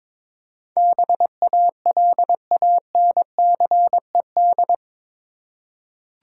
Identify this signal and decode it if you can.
Morse code, heard as a keyed tone, about 22 wpm, 700 Hz. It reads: BALANCED
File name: practice / 22wpm / Eight Letter Words - 22wpm